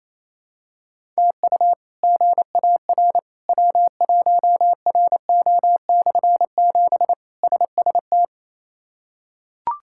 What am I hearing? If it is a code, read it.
Morse code, heard as a keyed tone, about 28 wpm, 700 Hz. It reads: TU GAR W1RO/7 SST
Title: K1USN SST QSO Practice - 28wpm